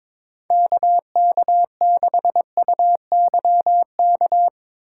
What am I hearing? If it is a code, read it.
Morse code, heard as a keyed tone, about 22 wpm, 700 Hz. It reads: KK6UYK
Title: USA Call Signs - 6 Characters - 2x Character Spacing - 22wpm